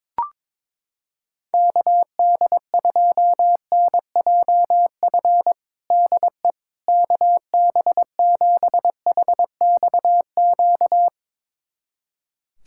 Morse code, heard as a keyed tone, about 23 wpm, 700 Hz. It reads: KD2NJF DE KB7HXQ
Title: USA Callsign DE Callsign - 6 Characters - 22wpm